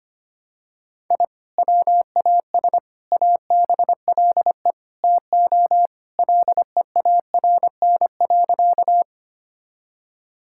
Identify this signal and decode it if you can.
Morse code, heard as a keyed tone, about 25 wpm, 700 Hz. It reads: I WAS ABLE TO LEARN.
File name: practice / 25wpm / Sentences with 4-Word Phrases - 25wpm